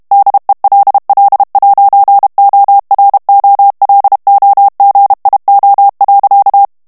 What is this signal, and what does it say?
Morse code, heard as a keyed tone, about 32 wpm, 800 Hz. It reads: DELL'OROLOGIO.
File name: ITALIEN texte clair 1 vitesse 30 mots minute